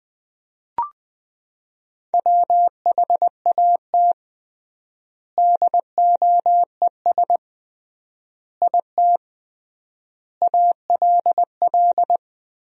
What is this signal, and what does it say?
Morse code, heard as a keyed tone, about 20 wpm, 700 Hz. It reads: WHAT DOES IT ALL
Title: 3x Word-Spacing - Sentences from Top 200 Words - 20wpm